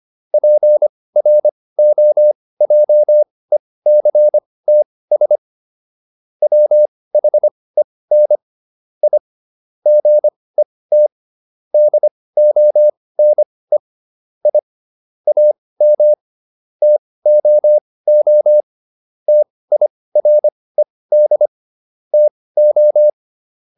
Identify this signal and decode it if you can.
Morse code, heard as a keyed tone, about 25 wpm, spaced as for 18 wpm, 600 Hz. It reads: PROJECTS WHEN I GET DONE I AM TOO TIRED TO